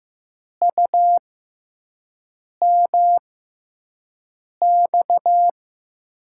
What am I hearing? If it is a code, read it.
Morse code, heard as a keyed tone, about 15 wpm, spaced as for 5 wpm, 700 Hz. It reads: UMX
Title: USA Call Signs - 5 Characters - 6x Character Spacing - 15wpm